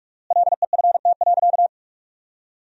Morse code, heard as a keyed tone, about 46 wpm, 700 Hz. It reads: LEFT.
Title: 4x Word-Spacing - Sentences from Top 300 Words - 45wpm